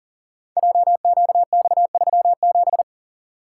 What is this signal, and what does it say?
Morse code, heard as a keyed tone, about 41 wpm, 700 Hz. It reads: JQX37
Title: Code Groups - 5 Letter:Numbers - 40wpm